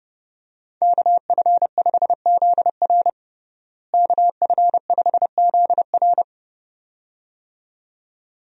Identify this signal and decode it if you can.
Morse code, heard as a keyed tone, about 30 wpm, 700 Hz. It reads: KF5ZR KF5ZR
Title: USA Call Signs - 5 Characters - Repeated 2x - 30wpm